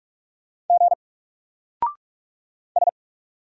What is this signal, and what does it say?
Morse code, heard as a keyed tone, about 45 wpm, 700 Hz. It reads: G S
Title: Single Letter-Number Alt 45wpm